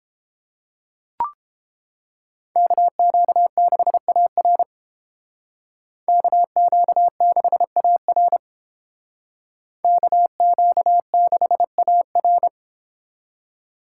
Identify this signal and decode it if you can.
Morse code, heard as a keyed tone, about 28 wpm, 700 Hz. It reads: KQ6AR KQ6AR KQ6AR